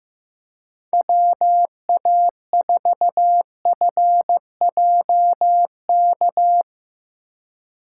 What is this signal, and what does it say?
Morse code, heard as a keyed tone, about 16 wpm, 700 Hz. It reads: WA4FJK
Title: Speed Racing - USA 2x3 Call Signs 15wpm